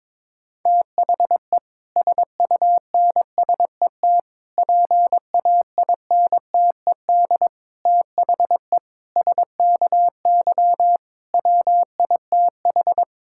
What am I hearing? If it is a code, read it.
Morse code, heard as a keyed tone, about 22 wpm, 700 Hz. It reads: THE SUNSET PAINTED THE SKY WITH
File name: Difficult Sentences with Hints - 22wpm